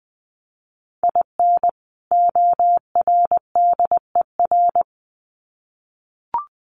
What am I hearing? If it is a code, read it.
Morse code, heard as a keyed tone, about 20 wpm, 700 Hz. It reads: IN ORDER
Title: Sets of 2 Words 20wpm